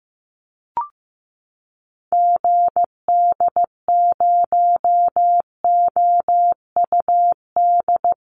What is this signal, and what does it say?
Morse code, heard as a keyed tone, about 15 wpm, 700 Hz. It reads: GD0OUD
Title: Call Signs Repeated 3x - 15wpm